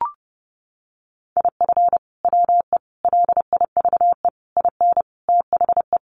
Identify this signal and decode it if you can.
Morse code, heard as a keyed tone, about 30 wpm, 700 Hz. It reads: IF WE LIVE IN THE
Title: Sentences from Top 1400 Words 30wpm